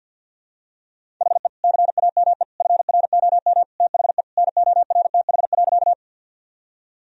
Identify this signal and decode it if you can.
Morse code, heard as a keyed tone, about 52 wpm, 700 Hz. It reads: HE CAME FROM THE NORTH.
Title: Sentences from Top 1000 Words - Review Part 1 - 50wpm